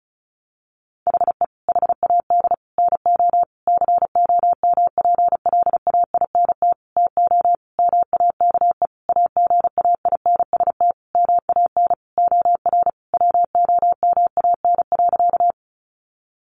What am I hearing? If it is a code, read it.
Morse code, heard as a keyed tone, about 34 wpm, 700 Hz. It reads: HE HAD NO COMPLAINT TO MAKE AGAINST MAN OR WOMAN.
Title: Sentences from Top 1400 Words 35wpm